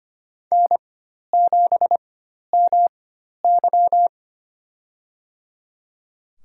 Morse code, heard as a keyed tone, about 25 wpm, spaced as for 12 wpm, 700 Hz. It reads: N7MY
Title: USA Call Signs - 4x Character Spacing - 25wpm